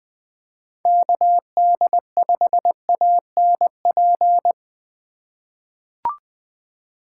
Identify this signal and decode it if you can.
Morse code, heard as a keyed tone, about 21 wpm, 700 Hz. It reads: KD5ANP
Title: USA Call Signs - 6 Characters - 3x Character Spacing - 20wpm